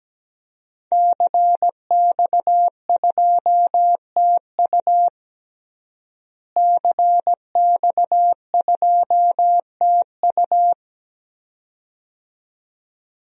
Morse code, heard as a keyed tone, about 17 wpm, 700 Hz. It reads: CX2TU CX2TU
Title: DXCC-100 Call Signs - 5 Characters - Repeated 2x - 17wpm